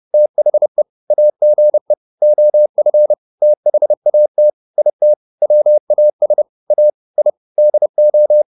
Morse code, heard as a keyed tone, about 30 wpm, 600 Hz. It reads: THE AGE OF THAT IT WAS A I DO